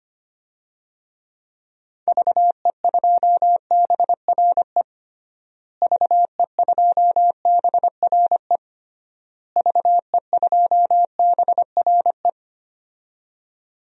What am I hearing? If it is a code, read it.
Morse code, heard as a keyed tone, about 25 wpm, 700 Hz. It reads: VE2BRE VE2BRE VE2BRE
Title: DXCC-100 Call Signs - 6 Characters - Repeated 3x - 25wpm